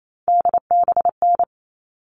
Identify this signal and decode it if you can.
Morse code, heard as a keyed tone, about 28 wpm, 700 Hz. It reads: DBN
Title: All Cut Numbers - 3-Digit 28wpm